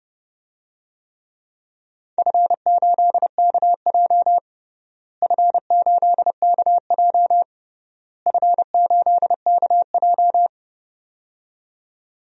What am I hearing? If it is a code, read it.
Morse code, heard as a keyed tone, about 30 wpm, 700 Hz. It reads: F8KJ F8KJ F8KJ
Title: DXCC-100 Call Signs - 4 Characters - Repeated 3x - 30wpm